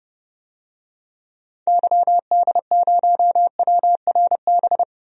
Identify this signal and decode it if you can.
Morse code, heard as a keyed tone, about 31 wpm, 700 Hz. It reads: YD0WRB
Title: DXCC-100 Call Signs - 6 Characters - Repeated 3x - 30wpm